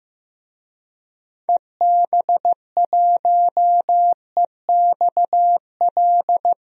Morse code, heard as a keyed tone, about 15 wpm, 700 Hz. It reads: EB1EXL
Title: DXCC-100 Call Signs - 6 Characters - 15wpm